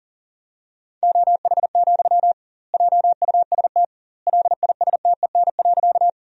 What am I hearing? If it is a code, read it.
Morse code, heard as a keyed tone, about 40 wpm, 700 Hz. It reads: OH, JUST LISTEN.
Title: 2x Word-Spacing - Sentences from Top 500 Words - 40wpm